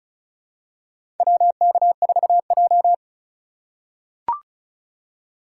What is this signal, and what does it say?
Morse code, heard as a keyed tone, about 36 wpm, 700 Hz. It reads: WK4J
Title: USA Call Signs - 2x Character Spacing - 35wpm